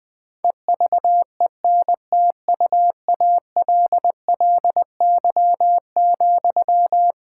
Morse code, heard as a keyed tone, about 20 wpm, 700 Hz. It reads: EVENTUALLY,
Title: Sentences from Top 1000 Words - Review Part 6 - 20wpm